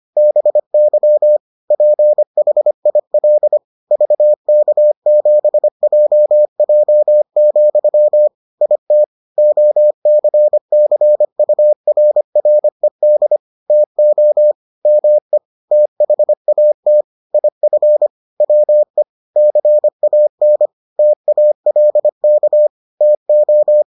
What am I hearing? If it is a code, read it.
Morse code, heard as a keyed tone, about 25 wpm, 600 Hz. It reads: BY PHIL VK7JJ, IT OCCURRED TO ME THAT IF WE CAN TALK TO